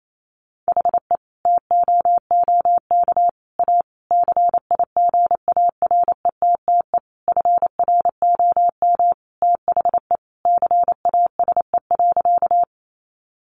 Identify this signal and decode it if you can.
Morse code, heard as a keyed tone, about 28 wpm, 700 Hz. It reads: HE TOOK A CIGARETTE FROM THE CASE.
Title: Sentences from Top 1000 Words 28wpm